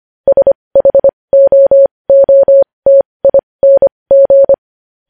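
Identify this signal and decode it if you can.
Morse code, heard as a keyed tone, about 25 wpm, spaced as for 20 wpm, 550 Hz. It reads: SHOOTING
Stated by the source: Morse practice sending news headlines at 20 wpm.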